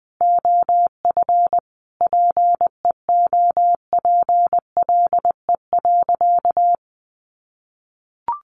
Morse code, heard as a keyed tone, about 20 wpm, 700 Hz. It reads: OF PEOPLE.
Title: Sentences from Top 1300 Words 20wpm